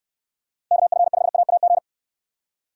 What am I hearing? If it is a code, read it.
Morse code, heard as a keyed tone, about 69 wpm, 700 Hz. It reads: KF5AAG